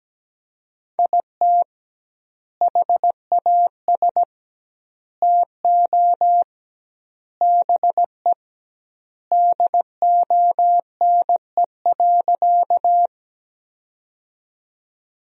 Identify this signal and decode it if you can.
Morse code, heard as a keyed tone, about 17 wpm, 700 Hz. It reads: IT HAS TO BE DONE.
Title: 2x Word-Spacing - Sentences from Top 500 Words - 17wpm